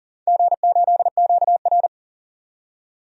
Morse code, heard as a keyed tone, about 42 wpm, 700 Hz. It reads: G8QR
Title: DXCC-100 Call Signs - 4 Characters - 40wpm